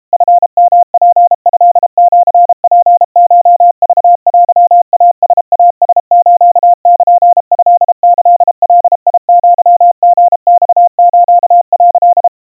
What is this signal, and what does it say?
Morse code (Japanese Wabun code), heard as a keyed tone, about 32 wpm, 700 Hz. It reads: チヨツトシツコクテイライラスルトキガアリマス」
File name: kawa / Morse sample / kamyu 120 cpm 700 Hz Crispy